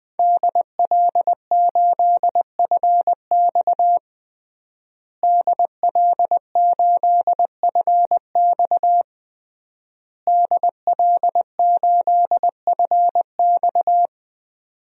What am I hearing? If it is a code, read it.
Morse code, heard as a keyed tone, about 20 wpm, 700 Hz. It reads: DL8FX DL8FX DL8FX
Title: DXCC-100 Call Signs - 5 Characters - Repeated 3x - 20wpm